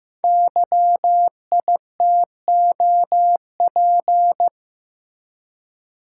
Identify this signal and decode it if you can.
Morse code, heard as a keyed tone, about 15 wpm, 700 Hz. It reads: YITOP
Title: Code Groups - 5 Letters - 15wpm